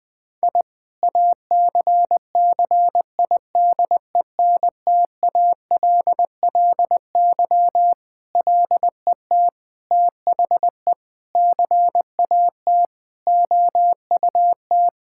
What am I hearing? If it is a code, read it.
Morse code, heard as a keyed tone, about 21 wpm, 700 Hz. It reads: I ACCIDENTALLY LET THE CAT OUT
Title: Sentences with 7-Word Phrases - Spoken Beginning - 20wpm